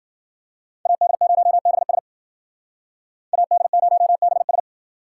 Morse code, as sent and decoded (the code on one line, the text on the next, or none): .- -.. ----- -... ... / .- -.. ----- -... ...
AD0BS AD0BS